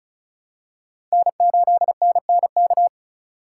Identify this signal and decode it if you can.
Morse code, heard as a keyed tone, about 34 wpm, 700 Hz. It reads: N8NNK
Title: USA Call Signs - 3x Character Spacing - 35wpm